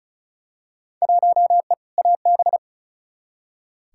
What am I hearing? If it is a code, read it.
Morse code, heard as a keyed tone, about 36 wpm, 700 Hz. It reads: 1E AB
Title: Field Day Exchanges - 35wpm